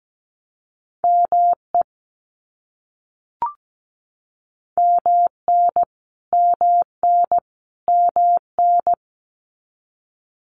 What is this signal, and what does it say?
Morse code, heard as a keyed tone, about 17 wpm, 700 Hz. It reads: ME MN MN MN
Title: US States Repeated 3x 17wpm